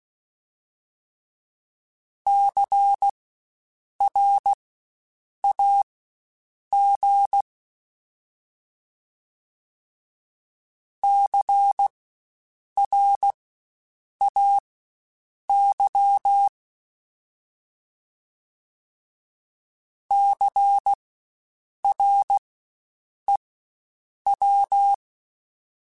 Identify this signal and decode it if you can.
Morse code, heard as a keyed tone, about 16 wpm, spaced as for 7 wpm, 800 Hz. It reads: CRAG CRAY CREW
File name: Spaced Alphas / Spaced Alphas mp3s / Cs sp @15